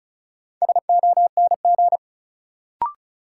Morse code, heard as a keyed tone, about 36 wpm, 700 Hz. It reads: SONG
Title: Sentences from Top 400 Words - New Word Drill - 35wpm